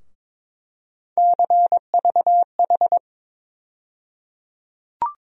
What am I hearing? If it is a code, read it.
Morse code, heard as a keyed tone, about 22 wpm, 700 Hz. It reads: CVH
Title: USA Call Sign Postfixes - 3 Letters 22wpm